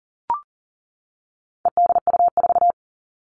Morse code, heard as a keyed tone, about 40 wpm, 700 Hz. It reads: EDU4
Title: All Cut Numbers - 4-Digit 40wpm